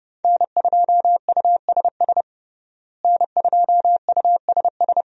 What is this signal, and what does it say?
Morse code, heard as a keyed tone, about 32 wpm, 700 Hz. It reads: N2USS N2USS
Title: USA Call Signs - 5 Characters - Repeated 2x - 30wpm